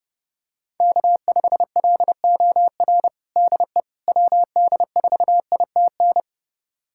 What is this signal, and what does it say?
Morse code, heard as a keyed tone, about 32 wpm, 700 Hz. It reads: K5LOR DE WD4ITN
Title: USA Callsign DE Callsign 30wpm